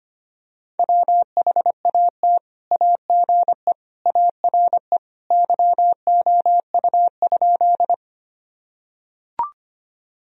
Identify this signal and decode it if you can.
Morse code, heard as a keyed tone, about 25 wpm, 700 Hz. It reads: WHAT AGE ARE YOU?
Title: Sentences from Top 1000 Words - Review Part 3 - 25wpm